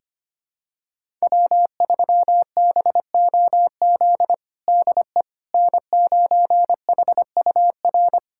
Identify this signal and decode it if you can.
Morse code, heard as a keyed tone, about 25 wpm, 700 Hz. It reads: W3BOZ DE N9HUR
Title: USA Callsign DE Callsign - 5 Characters - 25wpm